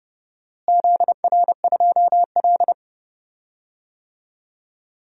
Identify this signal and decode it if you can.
Morse code, heard as a keyed tone, about 32 wpm, 700 Hz. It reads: ZR2L
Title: DXCC-100 Call Signs - 4 Characters - Repeated 2x - 30wpm